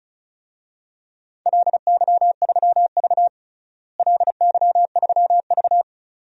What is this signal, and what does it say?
Morse code, heard as a keyed tone, about 34 wpm, 700 Hz. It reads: LY3V LY3V
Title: DXCC-100 Call Signs - 4 Characters - Repeated 2x - 35wpm